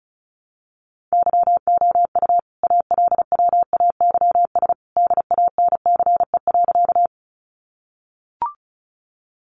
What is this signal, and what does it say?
Morse code, heard as a keyed tone, about 34 wpm, 700 Hz. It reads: YOU ALWAYS DANCE.
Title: Sentences from Top 700 Words 35wpm